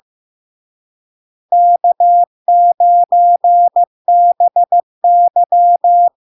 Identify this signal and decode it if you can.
Morse code, heard as a keyed tone, about 15 wpm, 700 Hz. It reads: K9BY
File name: LARK-FD-2022-Log-in-CW-15wpm-700-Hz